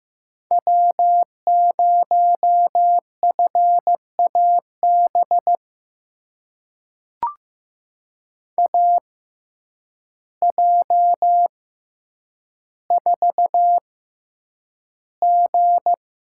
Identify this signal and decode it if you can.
Morse code, heard as a keyed tone, about 15 wpm, spaced as for 8 wpm, 700 Hz. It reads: W0FAB AJ4G
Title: USA Call Signs - 5 Characters - 6x Character Spacing - 15wpm